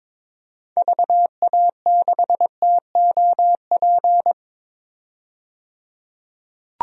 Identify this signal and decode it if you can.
Morse code, heard as a keyed tone, about 23 wpm, 700 Hz. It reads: VA6TOP
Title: DXCC-100 Call Signs - 6 Characters - 22wpm